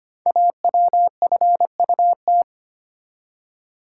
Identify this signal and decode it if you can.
Morse code, heard as a keyed tone, about 25 wpm, 700 Hz. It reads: AWFUT
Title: Code Groups - 5 Letters - 25wpm